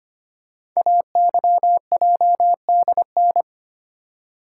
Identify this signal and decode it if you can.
Morse code, heard as a keyed tone, about 25 wpm, 700 Hz. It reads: AYJDN
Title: Code Groups - 5 Letters - 25wpm